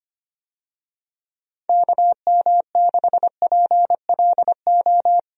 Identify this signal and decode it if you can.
Morse code, heard as a keyed tone, about 25 wpm, 700 Hz. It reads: KM6PLO
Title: USA Call Signs - 6 Characters - 25wpm